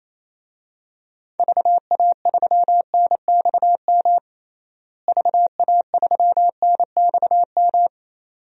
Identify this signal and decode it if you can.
Morse code, heard as a keyed tone, about 28 wpm, 700 Hz. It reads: VA3NXM VA3NXM
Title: DXCC-100 Call Signs - 6 Characters - Repeated 2x - 28wpm